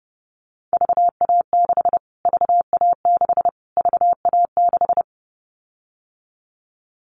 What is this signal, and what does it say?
Morse code, heard as a keyed tone, about 32 wpm, 700 Hz. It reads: VA6 VA6 VA6
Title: Callsign Prefixes for Top 25 DXCC - Repeated 3x - 30wpm